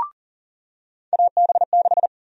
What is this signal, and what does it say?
Morse code, heard as a keyed tone, about 40 wpm, 700 Hz. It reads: AB6
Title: USA Call Sign Prefixes - 40wpm